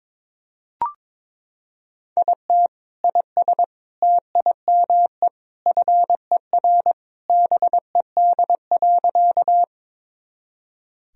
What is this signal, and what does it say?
Morse code, heard as a keyed tone, about 22 wpm, 700 Hz. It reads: IT IS TIME FER BED.
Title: Sentences from Top 500 Words and Abbreviations 22wpm